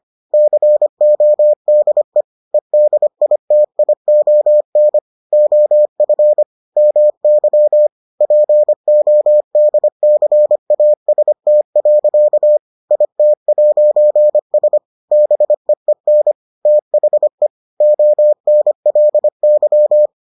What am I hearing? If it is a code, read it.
Morse code, heard as a keyed tone, about 25 wpm, 600 Hz. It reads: CODE EDITION OF MY PODCAST. IT'S BEEN THE ONLY